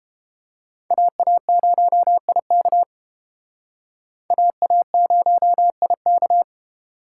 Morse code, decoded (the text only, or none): AA0IK AA0IK